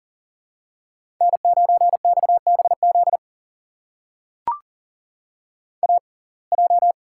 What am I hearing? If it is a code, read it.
Morse code, heard as a keyed tone, about 38 wpm, spaced as for 28 wpm, 700 Hz. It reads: N9XBZ AJ
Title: USA Call Signs - 6x Character Spacing - 40wpm